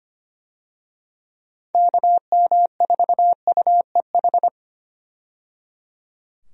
Morse code, heard as a keyed tone, about 25 wpm, 700 Hz. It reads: KM4UEH
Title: USA Call Signs - 6 Characters - 25wpm